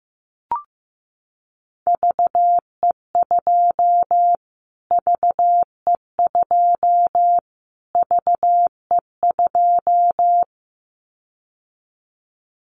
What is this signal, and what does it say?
Morse code, heard as a keyed tone, about 15 wpm, 700 Hz. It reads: VE2 VE2 VE2
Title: Callsign Prefixes for Top 25 DXCC - Repeated 3x - 15wpm